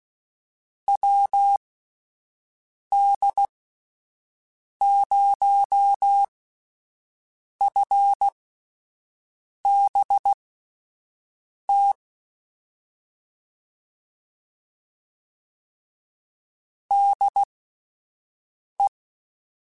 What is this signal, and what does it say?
Morse code, heard as a keyed tone, about 16 wpm, spaced as for 5 wpm, 800 Hz. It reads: WD0FBT DE
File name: (spaced) / QSOs Spaced-mp3s / WD0FBT 1 @10 sp